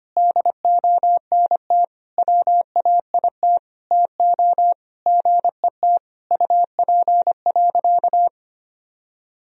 Morse code, heard as a keyed tone, about 25 wpm, 700 Hz. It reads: DONT WAIT TO GET UP.
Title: Sentences from Top 500 Words and Abbreviations 25wpm